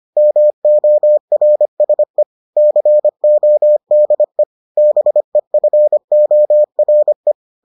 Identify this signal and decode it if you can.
Morse code, heard as a keyed tone, about 25 wpm, 600 Hz. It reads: MORSE CODE BEFORE